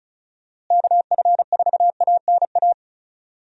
Morse code, heard as a keyed tone, about 34 wpm, 700 Hz. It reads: KF4ANA
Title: USA Call Signs - 6 Characters - Repeated 3x - 35wpm